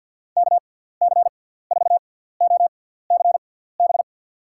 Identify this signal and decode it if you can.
Morse code, heard as a keyed tone, about 50 wpm, spaced as for 17 wpm, 700 Hz. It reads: KC4OCB
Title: USA Call Signs - 6 Characters - 6x Character Spacing - 50wpm